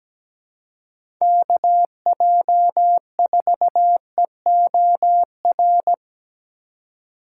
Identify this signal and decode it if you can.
Morse code, heard as a keyed tone, about 17 wpm, 700 Hz. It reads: KJ4EOR
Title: USA Call Signs - 6 Characters - 4x Character Spacing - 17wpm